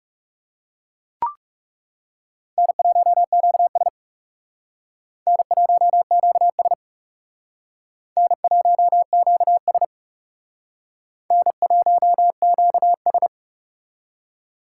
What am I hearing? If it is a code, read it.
Morse code, heard as a keyed tone, about 40 wpm, 700 Hz. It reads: N1QS N1QS N1QS N1QS